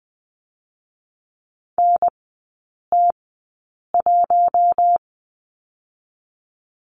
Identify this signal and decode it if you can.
Morse code, heard as a keyed tone, about 20 wpm, 700 Hz. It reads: N T 1
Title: 047 Lesson 9e - Number 4 - Three Characters 20wpm